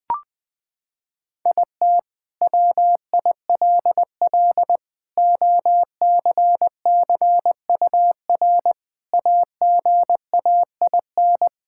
Morse code, heard as a keyed tone, about 21 wpm, 700 Hz. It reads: IT WILL OCCUR AGAIN
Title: Sentences from Top 1000 Words - Review Part 6 - 20wpm